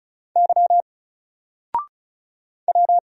Morse code, heard as a keyed tone, about 36 wpm, 700 Hz. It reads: Y W
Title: Single Letter-Number Alt 35wpm